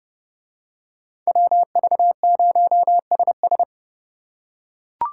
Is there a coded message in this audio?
Yes: Morse code from 1.3 to 3.6 s, heard as a keyed tone, about 30 wpm, 700 Hz. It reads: WV0SS